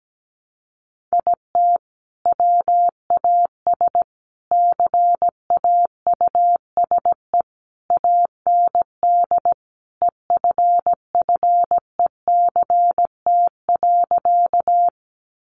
Morse code, heard as a keyed tone, about 17 wpm, 700 Hz. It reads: IT WAS CAUSE AND EFFECT.
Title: Sentences from Top 1000 Words 17wpm